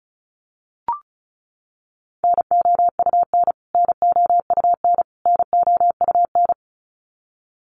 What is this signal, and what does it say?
Morse code, heard as a keyed tone, about 34 wpm, 700 Hz. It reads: NOUN NOUN NOUN
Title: Top 500 Words Repeated 35wpm